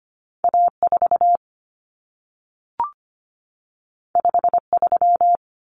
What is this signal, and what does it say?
Morse code, heard as a keyed tone, about 25 wpm, 700 Hz. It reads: A4 53
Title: Common Cut Numbers - 2-Digit 25wpm